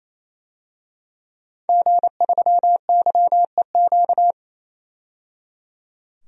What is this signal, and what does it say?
Morse code, heard as a keyed tone, about 28 wpm, 700 Hz. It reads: G3YEQ
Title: DXCC-100 Call Signs - 5 Characters - 28wpm